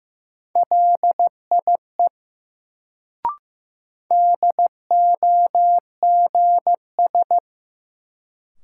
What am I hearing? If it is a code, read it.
Morse code, heard as a keyed tone, about 15 wpm, 700 Hz. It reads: LIE DOGS
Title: Sentences from Top 1600 Words - New Word Drill - 15wpm